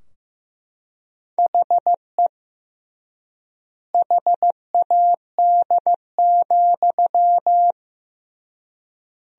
Morse code, heard as a keyed tone, about 15 wpm, 700 Hz. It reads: HE HAD,
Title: 3x Word-Spacing - Sentences from Top 500 Words - 15wpm